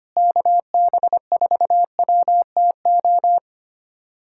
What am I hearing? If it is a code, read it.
Morse code, heard as a keyed tone, about 25 wpm, 700 Hz. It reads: KB4WTO
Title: USA Callsign DE Callsign - 6 Characters - 25wpm